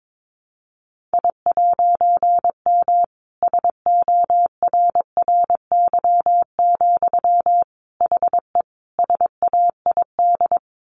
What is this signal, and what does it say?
Morse code, heard as a keyed tone, about 22 wpm, 700 Hz. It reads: I'M SORRY, HE SAID
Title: A Sound Of Thunder 22wpm